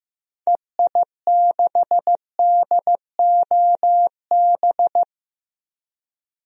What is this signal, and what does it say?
Morse code, heard as a keyed tone, about 15 wpm, 700 Hz. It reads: EI6DOB
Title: DXCC-100 Call Signs - 6 Characters - Repeated 3x - 15wpm